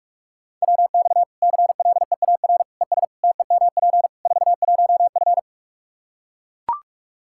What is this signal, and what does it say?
Morse code, heard as a keyed tone, about 41 wpm, 700 Hz. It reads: WX CLEAR ES TEMP 41F
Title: K1USN SST QSO Practice 45wpm